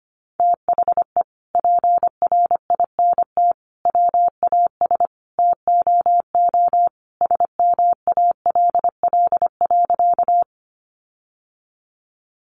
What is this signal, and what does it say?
Morse code, heard as a keyed tone, about 25 wpm, 700 Hz. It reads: THE PRINT WAS TOO SMALL.
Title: Sentences from Top 1400 Words 25wpm